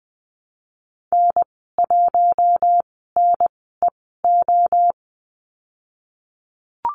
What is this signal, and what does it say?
Morse code, heard as a keyed tone, about 20 wpm, spaced as for 14 wpm, 700 Hz. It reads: N1NEO
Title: 055 Lesson 10g - Letter r - Call Signs 20wpm